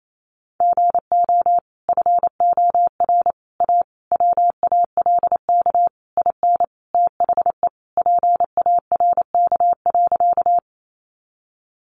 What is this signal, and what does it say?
Morse code, heard as a keyed tone, about 28 wpm, 700 Hz. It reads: GO FOR A WALK IN THE PARK.
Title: Sentences from Top 1300 Words 28wpm